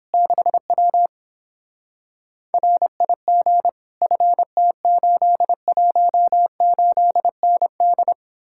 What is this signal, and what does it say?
Morse code, heard as a keyed tone, about 31 wpm, 700 Hz. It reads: 6W RIG FT818ND